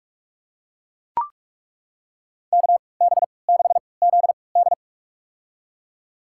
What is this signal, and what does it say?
Morse code, heard as a keyed tone, about 42 wpm, spaced as for 25 wpm, 700 Hz. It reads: KB6ZD
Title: USA Call Signs - 3x Character Spacing - 45wpm